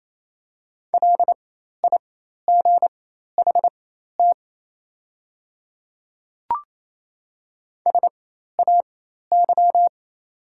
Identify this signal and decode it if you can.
Morse code, heard as a keyed tone, about 28 wpm, spaced as for 13 wpm, 700 Hz. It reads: LIGHT SAY
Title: 4x Character Spacing - Top 300 Words - 28wpm